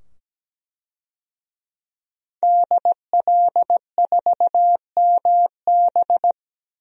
Morse code, heard as a keyed tone, about 17 wpm, 700 Hz. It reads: DL4MB
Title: DXCC-100 Call Signs - 5 Characters - 17wpm